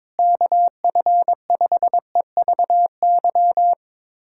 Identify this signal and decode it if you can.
Morse code, heard as a keyed tone, about 21 wpm, 700 Hz. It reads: KF5EVY
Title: USA Callsign DE Callsign 22wpm